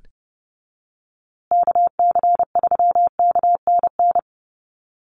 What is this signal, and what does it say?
Morse code, heard as a keyed tone, about 30 wpm, 700 Hz. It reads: KC3KNN
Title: USA Call Signs Repeated 3x 30wpm